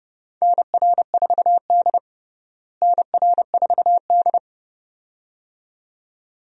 Morse code, heard as a keyed tone, about 30 wpm, 700 Hz. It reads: NR4D NR4D
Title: USA Call Signs - 4 Characters - Repeated 2x - 30wpm